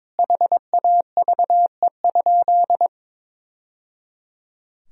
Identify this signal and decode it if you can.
Morse code, heard as a keyed tone, about 22 wpm, 700 Hz. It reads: HAVE?
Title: Sentences from Top 1000 Words - Review Part 3 - 22wpm